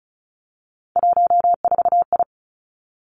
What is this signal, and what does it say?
Morse code, heard as a keyed tone, about 36 wpm, 700 Hz. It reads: 14I
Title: Three Challenging Letter-Number Combinations 35wpm